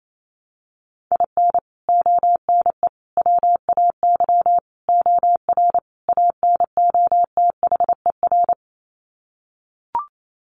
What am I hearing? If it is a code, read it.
Morse code, heard as a keyed tone, about 28 wpm, 700 Hz. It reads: IN ONE WAY OR ANOTHER
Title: Sets of 5 Words Encore 28wpm